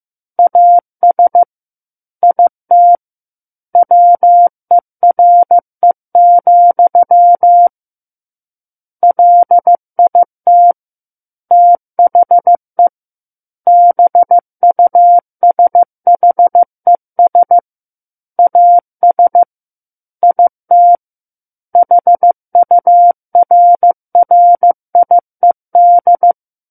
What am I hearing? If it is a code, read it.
Morse code, heard as a keyed tone, about 15 wpm, 700 Hz. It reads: AS IT WERE, LIT THE BUSHES AS IT HURRIED
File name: Chapter 6 - 15 WPM